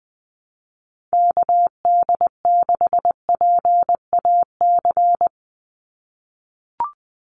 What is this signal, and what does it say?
Morse code, heard as a keyed tone, about 20 wpm, 700 Hz. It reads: KD6PAC
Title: USA Call Signs 20wpm